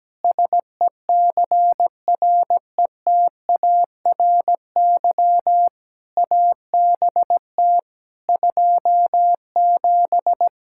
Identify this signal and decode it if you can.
Morse code, heard as a keyed tone, about 17 wpm, 700 Hz. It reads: SECRETARY ABT 27
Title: Practice QSO - Exchange 3 - More Info - 17wpm